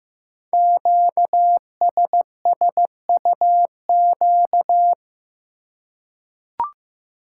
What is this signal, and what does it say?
Morse code, heard as a keyed tone, about 15 wpm, 700 Hz. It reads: QSSUQ
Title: Code Groups - 5 Letters - 15wpm